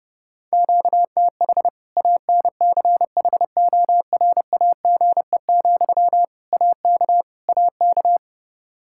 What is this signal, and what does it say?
Morse code, heard as a keyed tone, about 30 wpm, 700 Hz. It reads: QTH ANCHORAGE, AK AK
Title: Practice QSO - Exchange 1 - Beginning - 30wpm